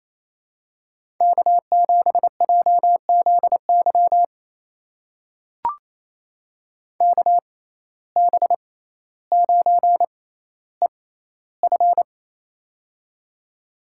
Morse code, heard as a keyed tone, about 28 wpm, spaced as for 14 wpm, 700 Hz. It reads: K7JZY KB9EF
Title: USA Call Signs - 5 Characters - 6x Character Spacing - 28wpm